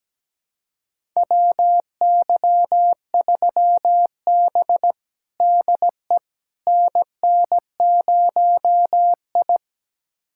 Morse code, heard as a keyed tone, about 17 wpm, 700 Hz. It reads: WY3B DE NN0I
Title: USA Callsign DE Callsign - 4 Characters - 17wpm